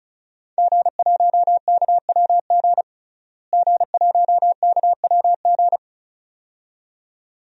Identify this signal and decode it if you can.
Morse code, heard as a keyed tone, about 36 wpm, 700 Hz. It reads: G1KWG G1KWG
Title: DXCC-100 Call Signs - 5 Characters - Repeated 2x - 35wpm